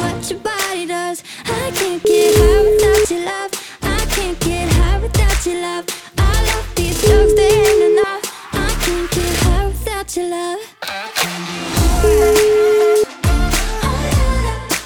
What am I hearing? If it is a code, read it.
Ringback tone (Europe, 425 Hz), with noise.